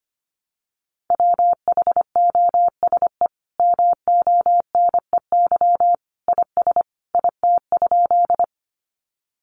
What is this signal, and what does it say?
Morse code, heard as a keyed tone, about 25 wpm, 700 Hz. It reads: WHOSE MONEY IS IT?
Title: Sentences from Top 1200 Words 25wpm